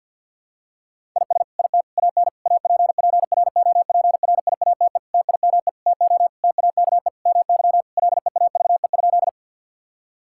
Morse code, heard as a keyed tone, about 50 wpm, 700 Hz. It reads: IS IT AN APPROPRIATE TIME TO TAKE MY LEAVE?
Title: Sentences from Top 1000 Words - Review Part 5 - 50wpm